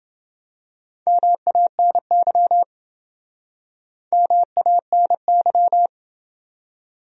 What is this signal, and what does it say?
Morse code, heard as a keyed tone, about 27 wpm, 700 Hz. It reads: MANY MANY